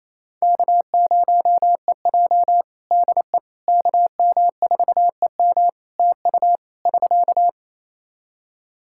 Morse code, heard as a keyed tone, about 28 wpm, 700 Hz. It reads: K0EJ DE KM4EM TU <SK>
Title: Practice QSO - Exchange 4 - Ending - 28wpm